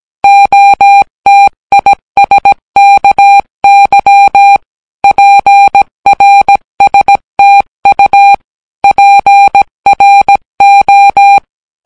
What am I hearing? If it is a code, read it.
Morse code, heard as a keyed tone, about 17 wpm, 800 Hz. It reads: OTISKY PRSTU PRO